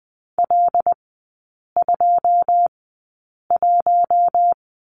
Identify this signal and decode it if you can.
Morse code, heard as a keyed tone, about 20 wpm, spaced as for 8 wpm, 690 Hz. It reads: L21
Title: 081 Lesson 14e - Number 2 - Three Characters 20wpm